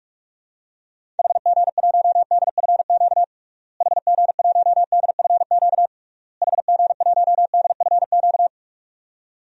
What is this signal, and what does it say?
Morse code, heard as a keyed tone, about 45 wpm, 700 Hz. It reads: HG1DFQ HG1DFQ HG1DFQ
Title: DXCC-100 Call Signs - 6 Characters - Repeated 3x - 45wpm